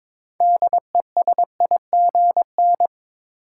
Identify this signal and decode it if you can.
Morse code, heard as a keyed tone, about 21 wpm, 700 Hz. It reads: DESIGN
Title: Sentences from Top 800 Words - New Word Drill - 22wpm